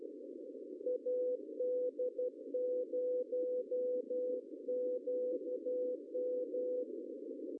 Morse code, heard as a keyed tone, about 12 wpm, 500 Hz, with noise. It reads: AD0QM
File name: CW / cqww-cw-k5tr